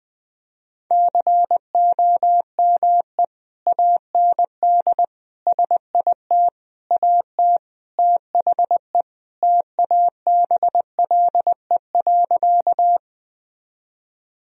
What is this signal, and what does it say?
Morse code, heard as a keyed tone, about 21 wpm, 700 Hz. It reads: COME AND SIT AT THE TABLE.
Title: Sentences from Top 1000 Words - Review Part 2 - 20wpm